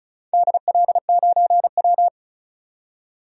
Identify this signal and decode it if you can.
Morse code, heard as a keyed tone, about 36 wpm, 700 Hz. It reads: DL9W
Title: DXCC-100 Call Signs - 4 Characters - 35wpm